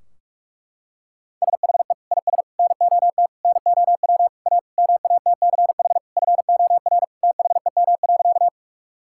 Morse code, heard as a keyed tone, about 42 wpm, 700 Hz. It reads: SHE IS NOT NOW A MATCH FOR THEM.
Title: Sentences from Top 1600 Words 45wpm